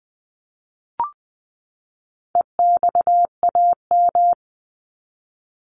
Morse code, heard as a keyed tone, about 20 wpm, 700 Hz. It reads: EXAM
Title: Top 501-750 Words 20wpm